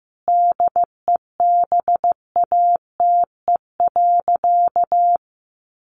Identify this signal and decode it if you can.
Morse code, heard as a keyed tone, about 15 wpm, 700 Hz. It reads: DEBATE.
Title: Sentences from Top 1100 Words 15wpm